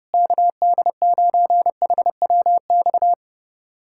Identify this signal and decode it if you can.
Morse code, heard as a keyed tone, about 30 wpm, 700 Hz. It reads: KD9HWX
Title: USA Callsign DE Callsign - 6 Characters - 30wpm